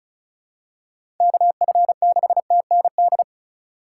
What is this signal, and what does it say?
Morse code, heard as a keyed tone, about 36 wpm, 700 Hz. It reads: KF6TND
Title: USA Call Signs - 6 Characters - 3x Character Spacing - 35wpm